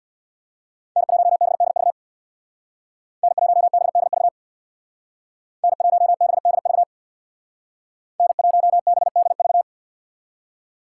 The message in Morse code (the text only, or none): N1BDV N1BDV N1BDV N1BDV